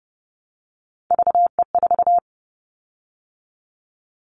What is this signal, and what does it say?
Morse code, heard as a keyed tone, about 30 wpm, 700 Hz. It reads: VE4
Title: Callsign Prefixes for Top 25 DXCC - 30wpm